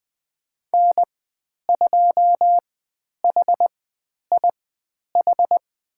Morse code, heard as a keyed tone, about 20 wpm, spaced as for 10 wpm, 700 Hz. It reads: N2HIH
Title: 102 Lesson 17g - Letter u - Call Signs 20wpm